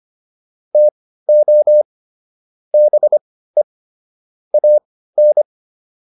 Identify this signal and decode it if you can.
Morse code, heard as a keyed tone, about 25 wpm, spaced as for 15 wpm, 600 Hz. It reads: TO BE AN